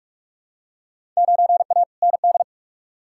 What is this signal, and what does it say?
Morse code, heard as a keyed tone, about 44 wpm, 700 Hz. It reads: 9A ND
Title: Field Day Exchanges Encore - 45wpm